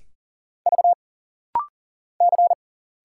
Morse code, heard as a keyed tone, about 40 wpm, 700 Hz. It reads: V C
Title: Single Letters - Rapid-Fire 40wpm